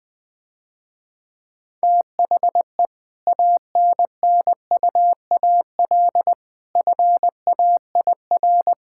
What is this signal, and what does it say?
Morse code, heard as a keyed tone, about 20 wpm, 700 Hz. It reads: THE ANNUAL FAIR
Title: Difficult Sentences with Hints - 20wpm